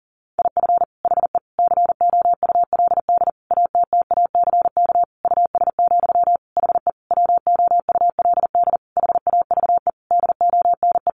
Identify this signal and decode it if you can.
Morse code, heard as a keyed tone, about 37 wpm, 700 Hz. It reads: IF HE COULD ATTACK US, HE WOULD HAVE DONE
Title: Sentences from Top 1500 Words 40wpm